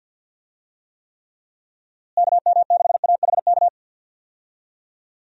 Morse code, heard as a keyed tone, about 52 wpm, 700 Hz. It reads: KM6AHK
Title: USA Call Signs - 6 Characters - Repeated 3x - 50wpm